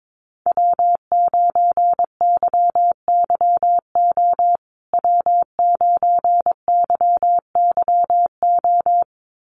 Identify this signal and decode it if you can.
Morse code, heard as a keyed tone, about 22 wpm, 700 Hz. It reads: W9YYO W9YYO
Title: USA Call Signs Repeated 2x 22wpm